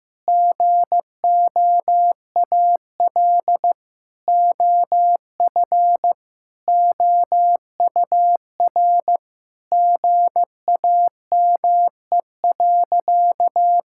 Morse code, heard as a keyed tone, about 15 wpm, 700 Hz. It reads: GOAL OF OUR GAME.
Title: Sentences from Top 1000 Words - Review Part 5 - 15wpm